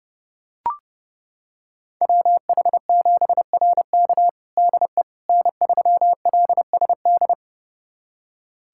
Morse code, heard as a keyed tone, about 32 wpm, 700 Hz. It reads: WH7RK DE N3LSD
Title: USA Callsign DE Callsign - 5 Characters - 30wpm